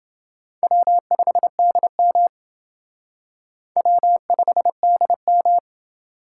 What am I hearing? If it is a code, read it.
Morse code, heard as a keyed tone, about 28 wpm, 700 Hz. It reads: W5DM W5DM